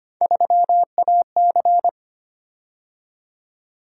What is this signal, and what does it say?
Morse code, heard as a keyed tone, about 25 wpm, 700 Hz. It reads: 3AC
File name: Three Letter-Number Combinations - 25wpm